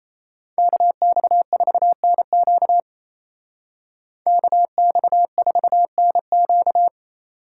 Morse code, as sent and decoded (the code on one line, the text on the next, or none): -.- -..- ....- -. --.- / -.- -..- ....- -. --.-
KX4NQ KX4NQ